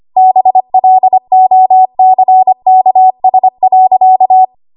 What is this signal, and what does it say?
Morse code, heard as a keyed tone, about 25 wpm, 750 Hz. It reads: BLOCKS.
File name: Archive / 25 WPM / 25WPM